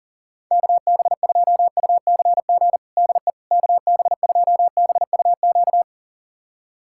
Morse code, heard as a keyed tone, about 40 wpm, 700 Hz. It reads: KB2UCG DE KB2BUQ
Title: USA Callsign DE Callsign - 6 Characters - 40wpm